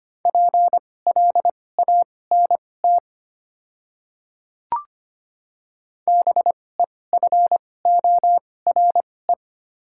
Morse code, heard as a keyed tone, about 25 wpm, spaced as for 18 wpm, 700 Hz. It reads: PLANT BEFORE
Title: 2x Character Spacing - Top 300 Words - 25wpm